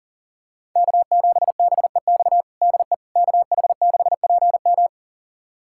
Morse code, heard as a keyed tone, about 40 wpm, 700 Hz. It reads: K7BEX DE KH6PM
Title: USA Callsign DE Callsign - 5 Characters - 40wpm